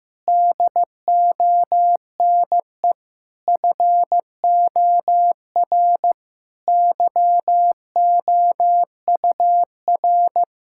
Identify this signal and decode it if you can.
Morse code, heard as a keyed tone, about 15 wpm, 700 Hz. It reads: DONE FOR YOUR
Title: Sentences from Top 1600 Words 15wpm